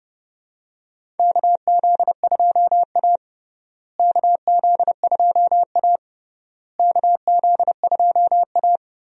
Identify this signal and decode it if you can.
Morse code, heard as a keyed tone, about 30 wpm, 700 Hz. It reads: KZ2A KZ2A KZ2A
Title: USA Call Signs - 4 Characters - Repeated 3x - 30wpm